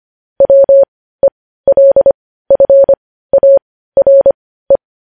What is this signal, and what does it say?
Morse code, heard as a keyed tone, about 25 wpm, spaced as for 15 wpm, 550 Hz. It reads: WELFARE
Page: news headlines in morse code at 15 wpm